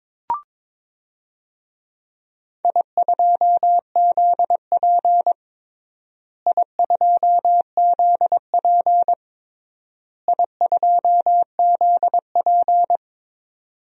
Morse code, heard as a keyed tone, about 22 wpm, 700 Hz. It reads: I2ZP I2ZP I2ZP
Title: DXCC-100 Call Signs - 4 Characters - Repeated 3x - 22wpm